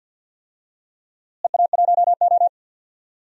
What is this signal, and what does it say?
Morse code, heard as a keyed tone, about 50 wpm, 700 Hz. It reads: EA1O
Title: DXCC-100 Call Signs - 4 Characters - Repeated 2x - 50wpm